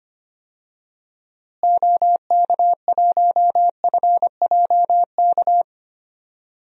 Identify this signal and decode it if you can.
Morse code, heard as a keyed tone, about 25 wpm, 700 Hz. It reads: OK1FJK
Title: DXCC-100 Call Signs - 6 Characters - Repeated 3x - 25wpm